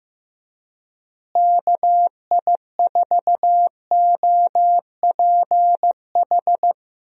Morse code, heard as a keyed tone, about 15 wpm, 700 Hz. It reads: KI4OPH
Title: USA Call Signs - 6 Characters - Repeated 2x - 15wpm